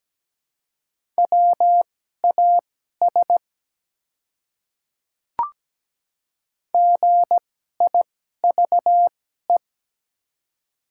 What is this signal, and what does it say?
Morse code, heard as a keyed tone, about 17 wpm, spaced as for 12 wpm, 700 Hz. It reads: WAS GIVE
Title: 2x Character Spacing - Top 200 Words - 17wpm